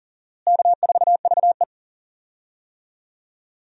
Morse code, heard as a keyed tone, about 40 wpm, 700 Hz. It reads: K4VE
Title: USA Call Signs - 4 Characters - Repeated 2x - 40wpm